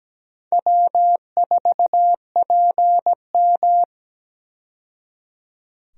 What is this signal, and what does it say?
Morse code, heard as a keyed tone, about 17 wpm, 700 Hz. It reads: W4PM
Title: USA Call Signs - 4 Characters - 17wpm